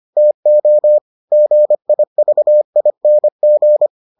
Morse code, heard as a keyed tone, about 25 wpm, 600 Hz. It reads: TO GIVING